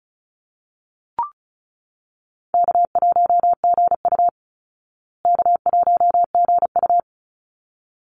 Morse code, heard as a keyed tone, about 34 wpm, 700 Hz. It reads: K1GU K1GU
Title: Easy Call Signs Repeated 2x - 35wpm